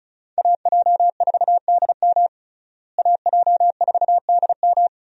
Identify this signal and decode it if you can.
Morse code, heard as a keyed tone, about 36 wpm, 700 Hz. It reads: AJ4DM AJ4DM
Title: USA Call Signs - 5 Characters - Repeated 2x - 35wpm